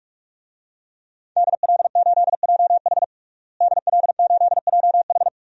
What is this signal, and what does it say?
Morse code, heard as a keyed tone, about 46 wpm, 700 Hz. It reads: DL8JH DL8JH
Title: DXCC-100 Call Signs - 5 Characters - Repeated 3x - 45wpm